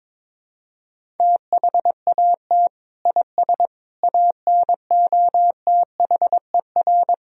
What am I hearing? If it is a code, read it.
Morse code, heard as a keyed tone, about 21 wpm, 700 Hz. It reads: THAT IS ANOTHER
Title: Sentences from Top 1000 Words - Review Part 1 - 22wpm